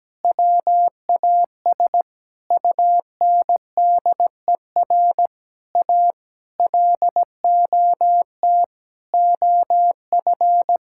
Morse code, heard as a keyed tone, about 17 wpm, 700 Hz. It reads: WAS UNDER A LOT OF
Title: Sentences from Top 1000 Words - Review Part 6 - 17wpm